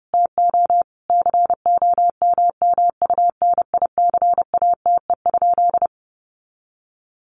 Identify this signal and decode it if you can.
Morse code, heard as a keyed tone, about 30 wpm, 700 Hz. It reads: TO COMMUNICATE?
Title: Sentences from Top 1300 Words 30wpm